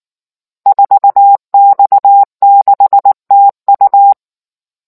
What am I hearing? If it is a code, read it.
Morse code, heard as a keyed tone, about 19 wpm, 800 Hz. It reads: 4X6TU